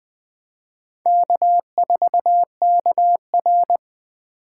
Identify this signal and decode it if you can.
Morse code, heard as a keyed tone, about 21 wpm, 700 Hz. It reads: K4KR